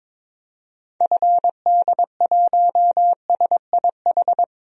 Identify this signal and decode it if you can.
Morse code, heard as a keyed tone, about 22 wpm, 700 Hz. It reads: FD1SIH
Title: DXCC-100 Call Signs - 6 Characters - Repeated 3x - 22wpm